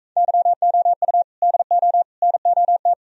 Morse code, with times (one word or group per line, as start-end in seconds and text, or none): 0.2-1.2 YOU
1.4-2.0 DO
2.2-2.9 NOT